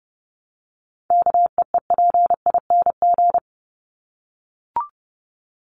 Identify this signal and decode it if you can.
Morse code, heard as a keyed tone, about 30 wpm, 700 Hz. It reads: KEEPING
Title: ING Suffix - Top 100 Words with Base Words 30wpm